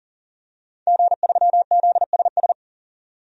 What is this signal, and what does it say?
Morse code, heard as a keyed tone, about 38 wpm, 700 Hz. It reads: G3ZSS
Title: DXCC-100 Call Signs - 5 Characters - 40wpm